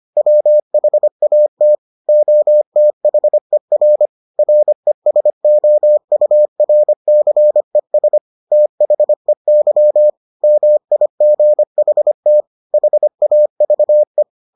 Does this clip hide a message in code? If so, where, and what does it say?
Yes: Morse code from 0.2 to 14.2 s, heard as a keyed tone, about 25 wpm, 600 Hz. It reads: WHAT OTHER RESOURCES THEY MIGHT HAVE